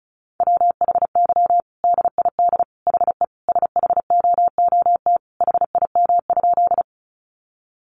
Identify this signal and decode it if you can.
Morse code, heard as a keyed tone, about 34 wpm, 700 Hz. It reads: WHY DID HE SHOOT HIM?
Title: Sentences from Top 1200 Words 35wpm